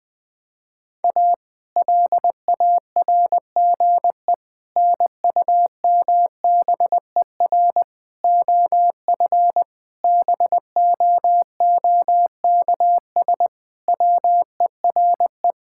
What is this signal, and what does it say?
Morse code, heard as a keyed tone, about 19 wpm, 700 Hz. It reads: A LARGE NUMBER OF BOOKS WERE
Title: Sentences with 4-Word Phrases - Spoken Beginning - 20wpm